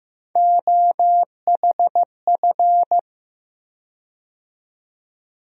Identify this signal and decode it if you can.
Morse code, heard as a keyed tone, about 16 wpm, 700 Hz. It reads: OHF
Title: USA Call Sign Postfixes - 3 Letters 15wpm